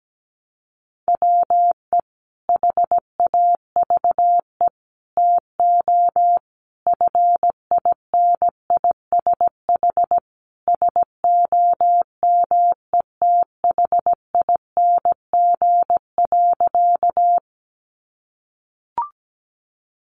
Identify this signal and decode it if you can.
Morse code, heard as a keyed tone, about 17 wpm, 700 Hz. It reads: WE HAVE TO FINISH SOMETHING.
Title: Sentences from Top 1200 Words 17wpm